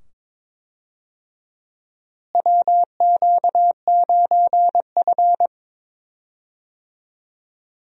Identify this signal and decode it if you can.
Morse code, heard as a keyed tone, about 22 wpm, 700 Hz. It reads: WQ9F
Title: USA Call Signs - 4 Characters - Repeated 2x - 22wpm